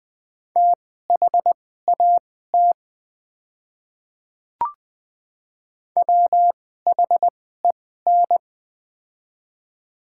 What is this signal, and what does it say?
Morse code, heard as a keyed tone, about 20 wpm, spaced as for 14 wpm, 700 Hz. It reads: THAT WHEN
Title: 2x Character Spacing - Top 100 Words - 20wpm